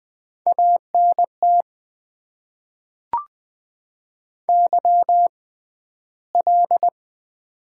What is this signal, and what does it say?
Morse code, heard as a keyed tone, about 20 wpm, spaced as for 12 wpm, 700 Hz. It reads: ANT YL
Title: Top 90 QSO Elements - 6x Character Spacing - 20wpm